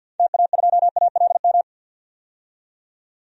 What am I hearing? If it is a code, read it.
Morse code, heard as a keyed tone, about 51 wpm, 700 Hz. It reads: TA2ALM
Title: DXCC-100 Call Signs - 6 Characters - 50wpm